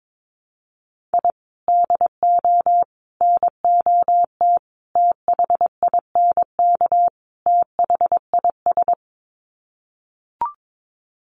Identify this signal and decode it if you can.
Morse code, heard as a keyed tone, about 22 wpm, 700 Hz. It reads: I DO NOT THINK THIS
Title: Sets of 5 Words Encore 22wpm